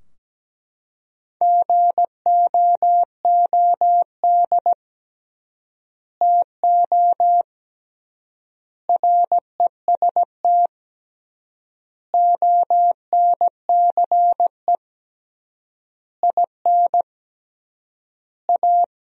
Morse code, heard as a keyed tone, about 17 wpm, 700 Hz. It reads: GOOD TO REST ONCE IN A
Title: 3x Word-Spacing - Sentences from Top 500 Words - 17wpm